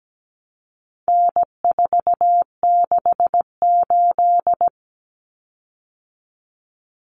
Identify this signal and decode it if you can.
Morse code, heard as a keyed tone, about 17 wpm, 700 Hz. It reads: N468
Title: Common Cut Numbers - 4-Digit 17wpm